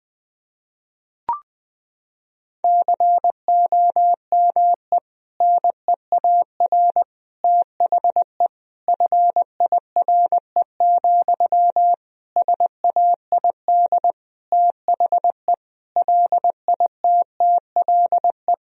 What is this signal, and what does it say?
Morse code, heard as a keyed tone, about 20 wpm, 700 Hz. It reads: COME NEAR THE FIRE, SAID THE LITTLE
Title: Sentences from Top 1000 Words - Review Part 1 - 20wpm